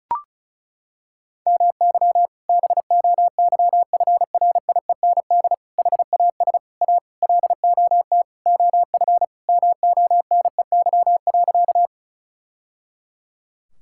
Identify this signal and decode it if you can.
Morse code, heard as a keyed tone, about 36 wpm, 700 Hz. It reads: MY BOYFRIEND HAS A LOT OF MONEY.
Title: Sentences from Top 1000 Words - Review Part 5 - 35wpm